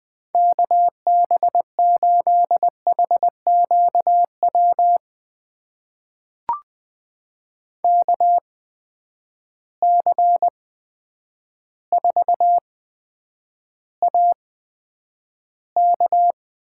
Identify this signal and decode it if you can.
Morse code, heard as a keyed tone, about 20 wpm, spaced as for 9 wpm, 700 Hz. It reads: KB8HQW KC4AK
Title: USA Call Signs - 6 Characters - 8x Character Spacing - 20wpm